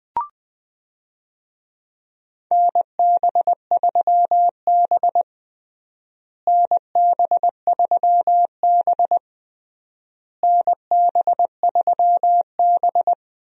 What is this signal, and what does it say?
Morse code, heard as a keyed tone, about 20 wpm, 700 Hz. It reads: NB3B NB3B NB3B
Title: USA Call Signs - 4 Characters - Repeated 3x - 20wpm